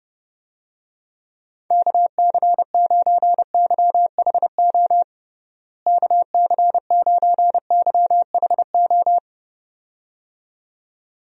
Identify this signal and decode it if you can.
Morse code, heard as a keyed tone, about 30 wpm, 700 Hz. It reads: KC9YHO KC9YHO
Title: USA Call Signs - 6 Characters - Repeated 2x - 30wpm